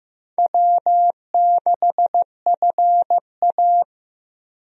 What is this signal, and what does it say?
Morse code, heard as a keyed tone, about 16 wpm, 700 Hz. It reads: W6FA
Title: USA Call Signs - 4 Characters - Repeated 3x - 15wpm